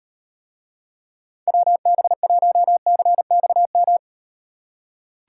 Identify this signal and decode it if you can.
Morse code, heard as a keyed tone, about 38 wpm, 700 Hz. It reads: WB1CXM